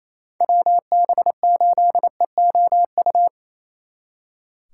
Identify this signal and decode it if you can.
Morse code, heard as a keyed tone, about 29 wpm, 700 Hz. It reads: WB8EOU
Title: Speed Racing - USA 2x3 Call Signs 28wpm